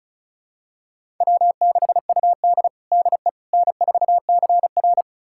Morse code, heard as a keyed tone, about 34 wpm, 700 Hz. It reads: W6UD DE N4CR
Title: USA Callsign DE Callsign - 4 Characters - 35wpm